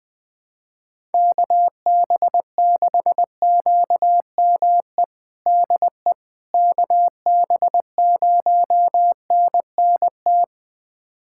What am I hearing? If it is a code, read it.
Morse code, heard as a keyed tone, about 20 wpm, 700 Hz. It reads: KB6QME DE KB0NNT
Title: USA Callsign DE Callsign - 6 Characters - 20wpm